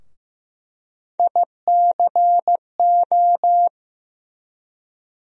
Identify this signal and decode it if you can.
Morse code, heard as a keyed tone, about 15 wpm, 700 Hz. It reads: ICO
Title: USA Call Sign Postfixes - 3 Letters 15wpm